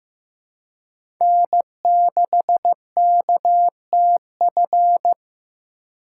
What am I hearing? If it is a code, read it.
Morse code, heard as a keyed tone, about 15 wpm, 700 Hz. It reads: N6KTF
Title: USA Call Signs - 5 Characters - 4x Character Spacing - 15wpm